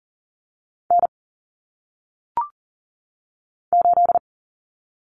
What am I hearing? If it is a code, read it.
Morse code, heard as a keyed tone, about 40 wpm, 700 Hz. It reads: N 8
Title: Common Cut Numbers - 1-Digit 40wpm